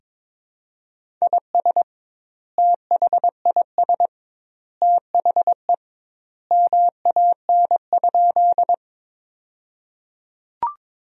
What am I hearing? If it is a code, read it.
Morse code, heard as a keyed tone, about 21 wpm, 700 Hz. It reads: IS THIS THE MAN?
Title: 2x Word-Spacing - Sentences from Top 200 Words - 22wpm